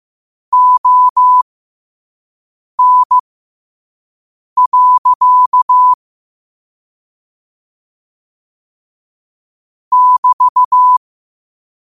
Morse code, heard as a keyed tone, about 15 wpm, spaced as for 5 wpm, 1000 Hz. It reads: ON. <BT>
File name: qotdfiles / 5wpm